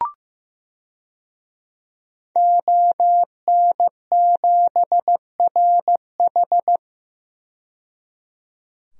Morse code, heard as a keyed tone, about 15 wpm, 700 Hz. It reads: ON7RH
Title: DXCC-100 Call Signs - 5 Characters - 15wpm